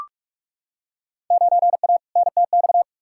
Morse code, heard as a keyed tone, about 46 wpm, 700 Hz. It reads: 9A NTX
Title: Field Day Exchanges Encore - 45wpm